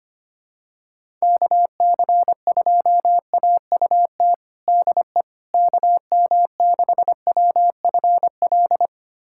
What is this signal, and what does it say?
Morse code, heard as a keyed tone, about 25 wpm, 700 Hz. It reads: KC2AUT DE KM6WFL
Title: USA Callsign DE Callsign - 6 Characters - 25wpm